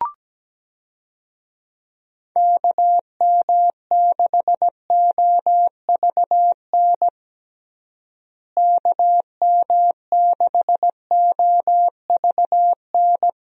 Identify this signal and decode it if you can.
Morse code, heard as a keyed tone, about 17 wpm, 700 Hz. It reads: KM6OVN KM6OVN
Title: USA Call Signs - 6 Characters - Repeated 2x - 17wpm